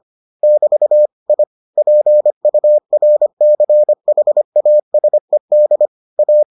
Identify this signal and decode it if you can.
Morse code, heard as a keyed tone, about 25 wpm, 600 Hz. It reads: <BT> I PURCHASED A